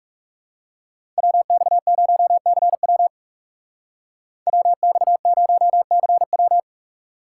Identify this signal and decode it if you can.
Morse code, heard as a keyed tone, about 37 wpm, 700 Hz. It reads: WX0CW WX0CW
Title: Speed Racing - USA Call Signs 30wpm